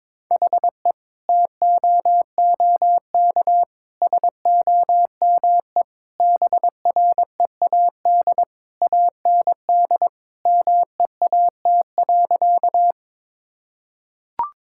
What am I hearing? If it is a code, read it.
Morse code, heard as a keyed tone, about 22 wpm, 700 Hz. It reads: HE TOOK SOME BREAD AND MEAT.
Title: Sentences from Top 1600 Words 22wpm